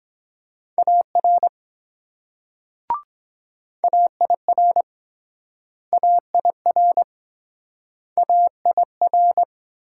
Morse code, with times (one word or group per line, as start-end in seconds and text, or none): 0.8-1.5 AR
3.8-4.8 AIR
5.9-7.0 AIR
8.2-9.4 AIR